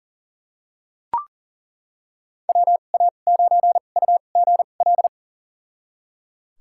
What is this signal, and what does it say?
Morse code, heard as a keyed tone, about 38 wpm, spaced as for 28 wpm, 700 Hz. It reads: WA9UGL
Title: USA Call Signs - 2x Character Spacing - 40wpm